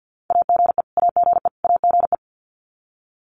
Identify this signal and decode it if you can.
Morse code, heard as a keyed tone, about 50 wpm, 700 Hz. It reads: AGE AGE AGE
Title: Top 500 Words Repeated 50wpm